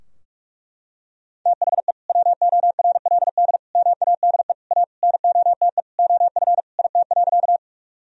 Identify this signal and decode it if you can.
Morse code, heard as a keyed tone, about 41 wpm, 700 Hz. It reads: THE WORLD MADE A NOTE OF IT.
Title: Sentences from Top 1000 Words - Review Part 2 - 45wpm